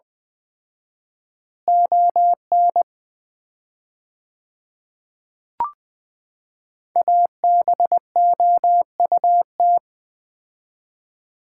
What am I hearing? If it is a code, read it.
Morse code, heard as a keyed tone, about 20 wpm, 700 Hz. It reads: ON ABOUT
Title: 4x Word-Spacing - Sets of 2 Words - 20wpm